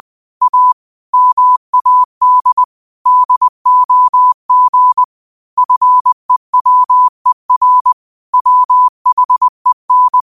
Morse code, heard as a keyed tone, about 20 wpm, 1000 Hz. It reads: A MAD DOG FEWER WHEN
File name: qotdfiles / 20wpm